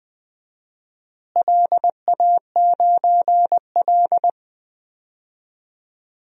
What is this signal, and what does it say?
Morse code, heard as a keyed tone, about 20 wpm, 700 Hz. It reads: LA9L
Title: DXCC-100 Call Signs - 4 Characters - Repeated 3x - 20wpm